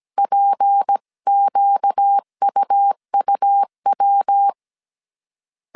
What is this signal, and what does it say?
Morse code, heard as a keyed tone, about 17 wpm, 800 Hz. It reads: PQUUW
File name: CW / DET / 1 6 1500C S A 16wpm 2635